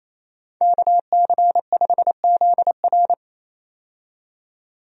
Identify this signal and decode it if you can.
Morse code, heard as a keyed tone, about 28 wpm, 700 Hz. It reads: KC5ZR
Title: USA Call Signs - 5 Characters - 28wpm